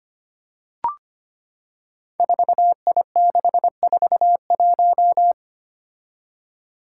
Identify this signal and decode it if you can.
Morse code, heard as a keyed tone, about 25 wpm, 700 Hz. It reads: 4I641
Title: Five Challenging Letter-Number Combinations 25wpm